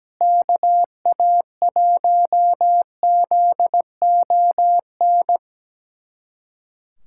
Morse code, heard as a keyed tone, about 17 wpm, 700 Hz. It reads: KA1ZON
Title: USA Callsign DE Callsign - 6 Characters - 17wpm